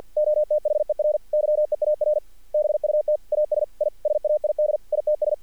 Morse code, heard as a keyed tone, about 50 wpm, 600 Hz. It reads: OTHER YEAR BUT AS I SAID ITS
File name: archived / CRSnet